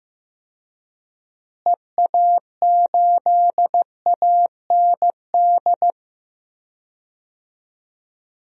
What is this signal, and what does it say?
Morse code, heard as a keyed tone, about 15 wpm, 700 Hz. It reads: EA8AND
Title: DXCC-100 Call Signs - 6 Characters - Repeated 2x - 15wpm